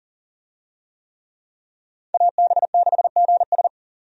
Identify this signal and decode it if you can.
Morse code, heard as a keyed tone, about 40 wpm, 700 Hz. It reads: AB6GS
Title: USA Call Signs - 5 Characters - 40wpm